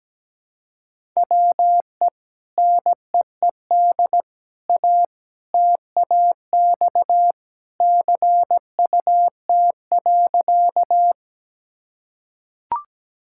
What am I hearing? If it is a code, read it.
Morse code, heard as a keyed tone, about 17 wpm, 700 Hz. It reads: WE NEED A TAX CUT.
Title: Sentences from Top 1000 Words - Review Part 5 - 17wpm